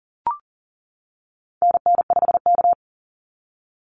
Morse code, heard as a keyed tone, about 40 wpm, 700 Hz. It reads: NN5K
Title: NA SOTA Activator Callsigns 40wpm